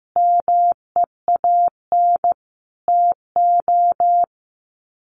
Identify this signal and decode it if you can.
Morse code, heard as a keyed tone, about 15 wpm, 700 Hz. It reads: MEAN TO
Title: Sets of 5 Words 15wpm